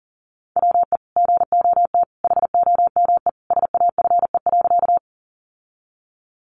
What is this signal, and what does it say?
Morse code, heard as a keyed tone, about 40 wpm, 700 Hz. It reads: WE GOT HOME SAFE.
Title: Sentences from Top 900 Words 40wpm